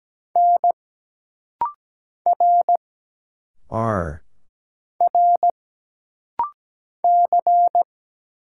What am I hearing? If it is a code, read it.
Morse code, heard as a keyed tone, about 17 wpm, 700 Hz. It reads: N R R C
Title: Single Letters - Rapid-Fire 17wpm